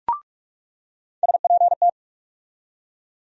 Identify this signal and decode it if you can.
Morse code, heard as a keyed tone, about 43 wpm, 700 Hz. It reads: SPT
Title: USA Call Sign Postfixes - 3 Letters 45wpm